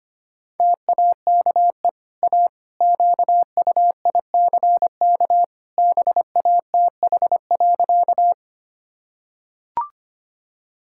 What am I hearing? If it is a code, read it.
Morse code, heard as a keyed tone, about 25 wpm, 700 Hz. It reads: TAKE A QUICK BATH.
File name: Easy Sentences with Hints - 25wpm